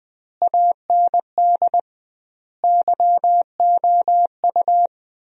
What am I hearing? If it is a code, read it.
Morse code, heard as a keyed tone, about 19 wpm, 700 Hz. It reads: AND YOU
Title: 2x Word-Spacing - Sentences from Top 400 Words - 20wpm